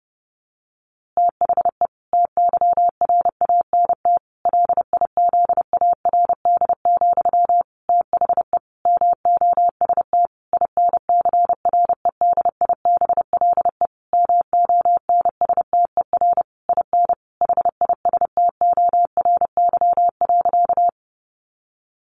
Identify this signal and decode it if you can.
Morse code, heard as a keyed tone, about 28 wpm, 700 Hz. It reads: THE TYRANT LIZARD, THE MOST INCREDIBLE MONSTER IN HISTORY.
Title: A Sound Of Thunder 30wpm